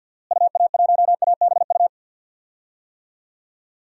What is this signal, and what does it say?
Morse code, heard as a keyed tone, about 49 wpm, 700 Hz. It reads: UA1ABU
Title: DXCC-100 Call Signs - 6 Characters - Repeated 2x - 50wpm